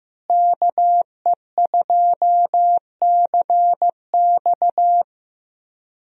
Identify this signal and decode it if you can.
Morse code, heard as a keyed tone, about 15 wpm, 700 Hz. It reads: KE2CX
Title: USA Call Signs - 5 Characters - Repeated 2x - 15wpm